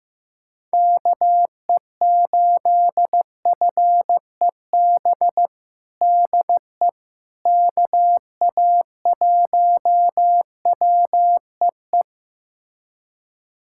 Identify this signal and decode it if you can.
Morse code, heard as a keyed tone, about 15 wpm, 700 Hz. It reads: KE8FEB DE KA1WEE
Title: USA Callsign DE Callsign - 6 Characters - 15wpm